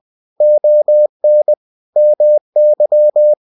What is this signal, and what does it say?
Morse code, heard as a keyed tone, about 20 wpm, 600 Hz. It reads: ON MY
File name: cw-books / winnie-the-pooh / winnie-20wpm-page-0047